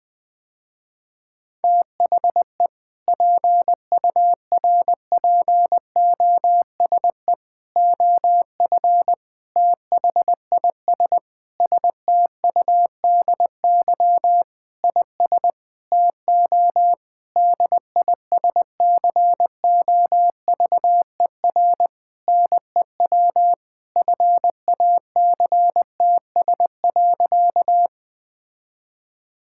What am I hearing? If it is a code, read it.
Morse code, heard as a keyed tone, about 20 wpm, 700 Hz. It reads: THE PURPOSE OF THIS STUDY IS TO DISCOVER NEW FACTS.
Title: Sentences with 5-Word Phrases - Spoken Beginning - 20wpm